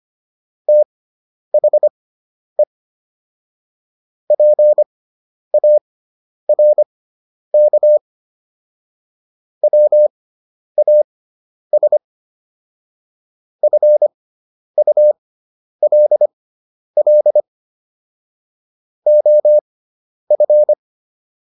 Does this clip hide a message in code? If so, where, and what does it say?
Yes: Morse code from 0.7 to 20.7 s, heard as a keyed tone, about 25 wpm, spaced as for 10 wpm, 600 Hz. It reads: THE PARK WAS FULL OF